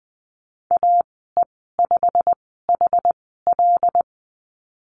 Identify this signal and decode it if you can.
Morse code, heard as a keyed tone, about 19 wpm, spaced as for 14 wpm, 700 Hz. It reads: AE5HL
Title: 088 Lesson 15g - Number 5 - Call Signs 20wpm